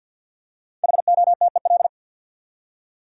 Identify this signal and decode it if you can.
Morse code, heard as a keyed tone, about 47 wpm, 700 Hz. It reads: HOTEL
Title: Five Letter Words - 50wpm